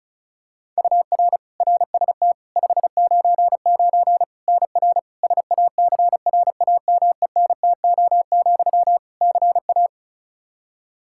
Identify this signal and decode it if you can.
Morse code, heard as a keyed tone, about 36 wpm, 700 Hz. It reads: UR RST 599 NR SACRAMENTO, CA
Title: Practice QSO - Exchange 1 - Beginning - 35wpm